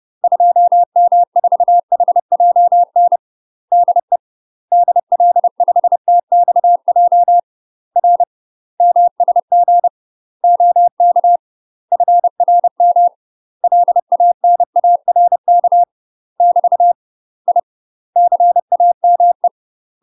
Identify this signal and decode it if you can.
Morse code, heard as a keyed tone, about 30 wpm, 700 Hz. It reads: 2M4HJN DE DL5TXJ R MSG OK FRM LANARK <BT> I CAME
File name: UK MASTER QSOs TRAINING / 30WPM / UK MASTER QSOs 0013